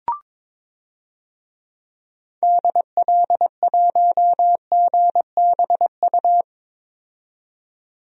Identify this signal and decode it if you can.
Morse code, heard as a keyed tone, about 22 wpm, 700 Hz. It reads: DL1GBU
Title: DXCC-100 Call Signs - 6 Characters - 22wpm